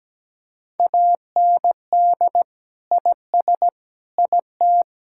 Morse code, heard as a keyed tone, about 17 wpm, 700 Hz. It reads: AND IS IT
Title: Sentences from Top 1600 Words 17wpm